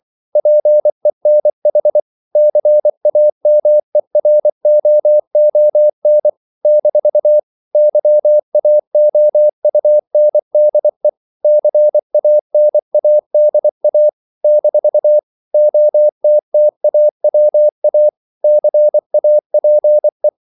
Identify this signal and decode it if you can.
Morse code, heard as a keyed tone, about 24 wpm, 600 Hz. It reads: PENH CAMEROON - YAOUNDE CANADA - OTTAWA CAPE
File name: CW / Practice / NationalCapitals24